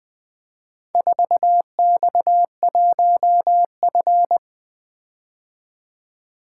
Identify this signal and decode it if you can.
Morse code, heard as a keyed tone, about 20 wpm, 700 Hz. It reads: 4X1F
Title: Four Letter-Number Combinations - 20wpm